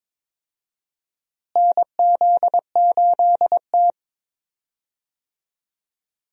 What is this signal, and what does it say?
Morse code, heard as a keyed tone, about 22 wpm, 700 Hz. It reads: NZ8T
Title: USA Call Signs - 4 Characters - Repeated 3x - 22wpm